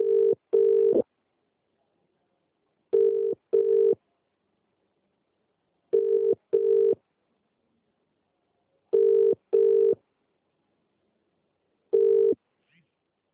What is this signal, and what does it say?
Telephone call progress tone (UK), heard as a telephone network's tone, about 400, 450 Hz, with speech. Ringback tone.